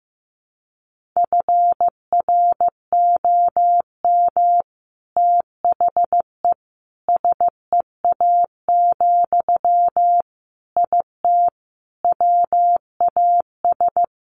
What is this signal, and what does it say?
Morse code, heard as a keyed tone, about 14 wpm, 700 Hz. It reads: FROM THE SEA, IT WAS
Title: Sentences from Top 1500 Words 15wpm